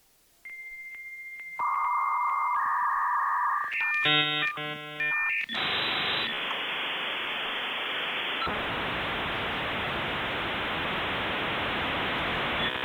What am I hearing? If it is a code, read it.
Dial-up modem handshake.
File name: Sam Krauter-56K Modem Verbindung